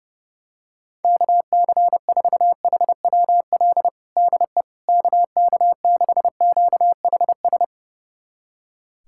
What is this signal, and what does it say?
Morse code, heard as a keyed tone, about 30 wpm, 700 Hz. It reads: KC4HWL DE KK6QHS
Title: USA Callsign DE Callsign 30wpm